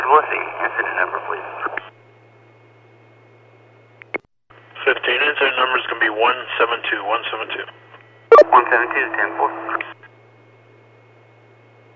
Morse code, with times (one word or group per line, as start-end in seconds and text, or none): none